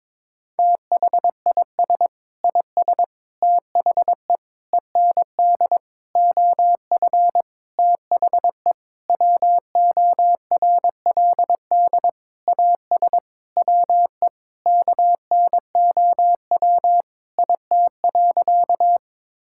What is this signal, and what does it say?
Morse code, heard as a keyed tone, about 21 wpm, 700 Hz. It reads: THIS IS THE END OF THE WORLD AS WE KNOW IT.
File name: Sentences with 9-Word Phrases - 22wpm